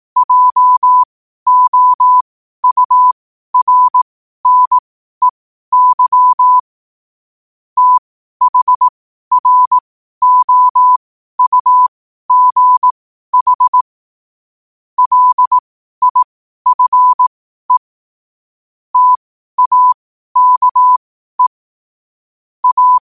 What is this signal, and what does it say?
Morse code, heard as a keyed tone, about 18 wpm, spaced as for 12 wpm, 1000 Hz. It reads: JOURNEY THROUGH LIFE TAKE A
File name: qotdfiles / 12wpm